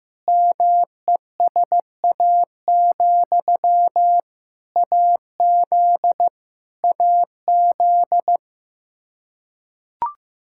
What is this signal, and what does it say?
Morse code, heard as a keyed tone, about 16 wpm, 700 Hz. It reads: MESA, AZ AZ
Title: Practice QSO - Exchange 1 - Beginning - 15wpm